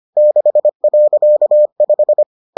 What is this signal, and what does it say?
Morse code, heard as a keyed tone, about 25 wpm, 600 Hz. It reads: 6.5